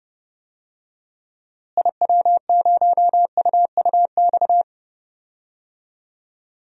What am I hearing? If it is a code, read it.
Morse code, heard as a keyed tone, about 30 wpm, 700 Hz. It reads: IW0UUX
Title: DXCC-100 Call Signs - 6 Characters - Repeated 2x - 30wpm